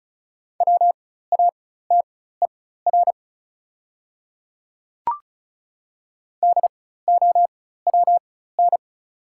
Morse code, heard as a keyed tone, about 36 wpm, spaced as for 16 wpm, 700 Hz. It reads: WATER DOWN
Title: 4x Character Spacing - Top 100 Words - 35wpm